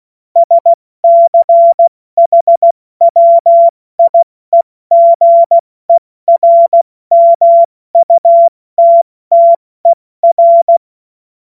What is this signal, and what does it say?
Morse code, heard as a keyed tone, about 16 wpm, 680 Hz. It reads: SCHWIEGERMUTTER